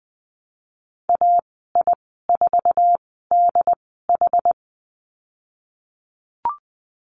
Morse code, heard as a keyed tone, about 20 wpm, spaced as for 15 wpm, 700 Hz. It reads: AI4DH
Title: 140 Lesson 23g - Letter f - Call Signs 20wpm